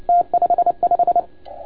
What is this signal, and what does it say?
Morse code, heard as a keyed tone, about 29 wpm, 690 Hz. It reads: T55
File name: Archiv / CUC287 / OK1FGD CUC 287